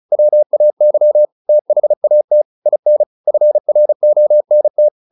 Morse code, heard as a keyed tone, about 35 wpm, 600 Hz. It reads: WAY THAT IN FRONT